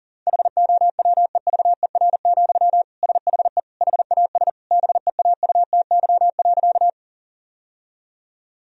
Morse code, heard as a keyed tone, about 40 wpm, 700 Hz. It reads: HOWEVER, SHE HAS BEAUTY.
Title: Sentences from Top 1000 Words - Review Part 2 - 40wpm